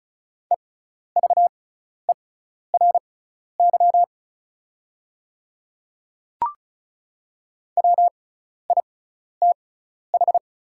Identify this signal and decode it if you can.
Morse code, heard as a keyed tone, about 36 wpm, spaced as for 12 wpm, 700 Hz. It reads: EVERY WITH
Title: 6x Character Spacing - Top 200 Words - 35wpm